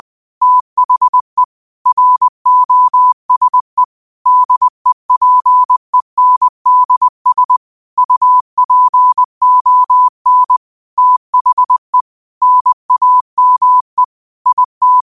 Morse code, heard as a keyed tone, about 20 wpm, 1000 Hz. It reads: THE ROSE DEPENDS UPON THE NAME IT